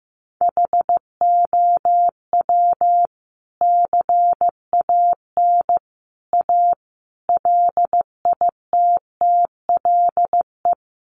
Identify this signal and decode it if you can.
Morse code, heard as a keyed tone, about 14 wpm, 700 Hz. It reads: HOW CAN A LITTLE
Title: Sentences from Top 500 Words 15wpm